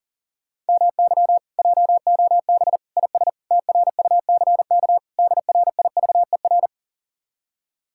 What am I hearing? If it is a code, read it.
Morse code, heard as a keyed tone, about 52 wpm, 700 Hz. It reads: MY JOB IS TRUCK DRIVER